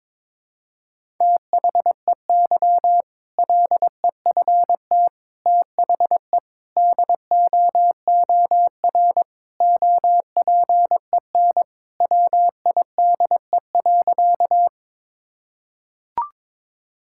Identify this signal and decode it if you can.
Morse code, heard as a keyed tone, about 22 wpm, 700 Hz. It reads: THEY LEFT THE DOOR OPEN WIDE.
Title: Sentences from Top 1000 Words - Review Part 6 - 22wpm